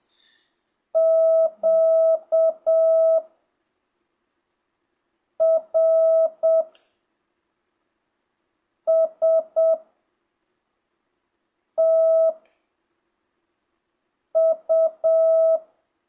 Morse code, heard as a keyed tone, about 7 wpm, spaced as for 3 wpm, 650 Hz. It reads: QRSTU
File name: jh8cbh / 2025cw